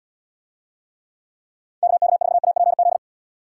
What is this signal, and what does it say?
Morse code, heard as a keyed tone, about 73 wpm, 700 Hz. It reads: KK4APZ